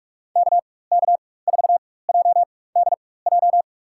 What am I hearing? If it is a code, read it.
Morse code, heard as a keyed tone, about 47 wpm, spaced as for 21 wpm, 700 Hz. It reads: KK4JDJ
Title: USA Call Signs - 4x Character Spacing - 45wpm